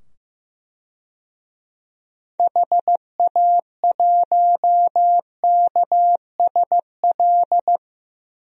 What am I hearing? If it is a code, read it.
Morse code, heard as a keyed tone, about 15 wpm, 700 Hz. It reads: HA1KSL
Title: DXCC-100 Call Signs - 6 Characters - 15wpm